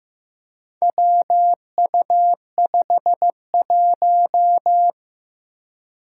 Morse code, heard as a keyed tone, about 15 wpm, 700 Hz. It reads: WU51
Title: Four Challenging Letter-Number Combinations 15wpm